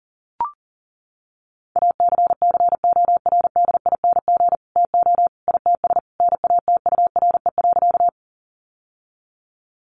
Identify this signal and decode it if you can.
Morse code, heard as a keyed tone, about 40 wpm, 700 Hz. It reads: ACCORDING TO ITS NATURE.
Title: Sentences from Top 700 Words 40wpm